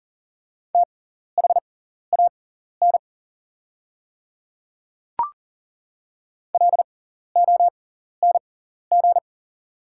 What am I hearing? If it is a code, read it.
Morse code, heard as a keyed tone, about 37 wpm, spaced as for 14 wpm, 690 Hz. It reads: THAN LONG
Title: 6x Character Spacing - Top 100 Words - 40wpm